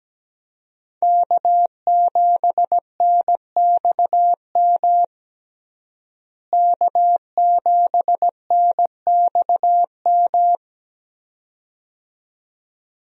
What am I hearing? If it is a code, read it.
Morse code, heard as a keyed tone, about 17 wpm, 700 Hz. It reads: K7NXM K7NXM
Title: USA Call Signs - 5 Characters - Repeated 2x - 17wpm